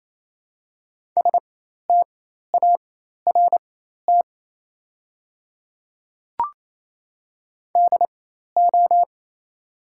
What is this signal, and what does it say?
Morse code, heard as a keyed tone, about 28 wpm, spaced as for 13 wpm, 700 Hz. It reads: START DO
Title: 4x Character Spacing - Top 500 Words - 28wpm